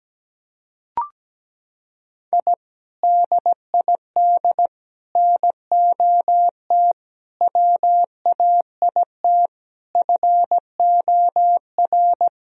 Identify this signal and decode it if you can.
Morse code, heard as a keyed tone, about 17 wpm, 700 Hz. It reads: I DID NOT WAIT FOR
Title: Sentences from Top 1000 Words - Review Part 2 - 17wpm